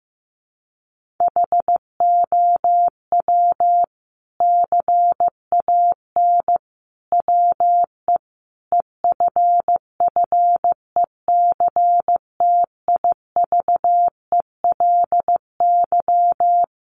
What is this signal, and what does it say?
Morse code, heard as a keyed tone, about 15 wpm, 700 Hz. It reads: HOW CAN WE EFFECTIVELY
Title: Sentences from Top 1400 Words 15wpm